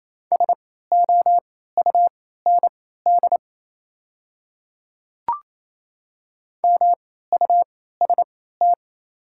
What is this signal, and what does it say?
Morse code, heard as a keyed tone, about 27 wpm, spaced as for 16 wpm, 700 Hz. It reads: SOUND MUST
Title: 3x Character Spacing - Top 200 Words - 28wpm